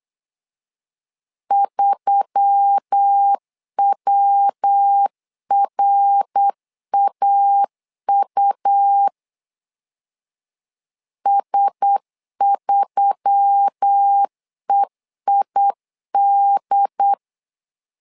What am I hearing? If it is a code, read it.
Morse code, heard as a keyed tone, about 8 wpm, 800 Hz. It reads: 3WRAU S3EID
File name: LEZIONI / 4 Q1S3 8wpm 3037